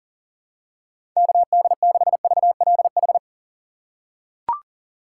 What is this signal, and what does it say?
Morse code, heard as a keyed tone, about 40 wpm, 700 Hz. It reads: KD6VLH
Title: USA Call Signs - 2x Character Spacing - 40wpm